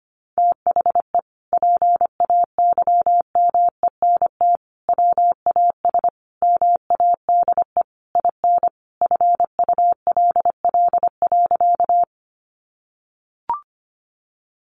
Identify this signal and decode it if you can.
Morse code, heard as a keyed tone, about 25 wpm, 700 Hz. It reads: THE PAYMENT WAS MADE IN FULL.
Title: Sentences from Top 1200 Words 25wpm